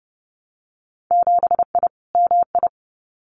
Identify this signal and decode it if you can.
Morse code, heard as a keyed tone, about 30 wpm, 700 Hz. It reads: 7I MI
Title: Winter Field Day Exchanges Encore 30wpm